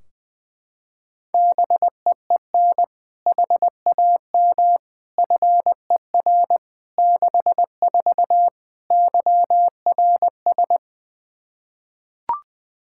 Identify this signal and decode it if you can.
Morse code, heard as a keyed tone, about 20 wpm, 700 Hz. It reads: BEEN HAM FER 64 YRS
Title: K1USN SST QSO Practice 20wpm